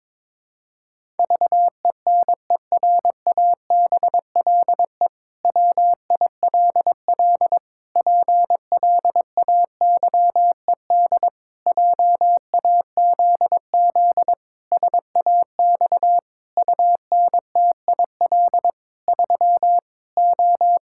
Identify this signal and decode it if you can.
Morse code, heard as a keyed tone, about 21 wpm, 700 Hz. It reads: VENERABLE WILL PLAYED JAZZ SAX UNTIL 3 O
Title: Pangram Challenge v2 - 22wpm